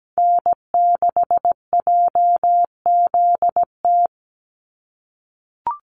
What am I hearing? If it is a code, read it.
Morse code, heard as a keyed tone, about 17 wpm, 700 Hz. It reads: N6JZT
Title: NA SOTA Chaser Callsigns 17wpm